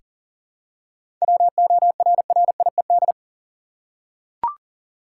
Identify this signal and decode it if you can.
Morse code, heard as a keyed tone, about 40 wpm, 700 Hz. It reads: WORRIED
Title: Seven Letter Words - 40wpm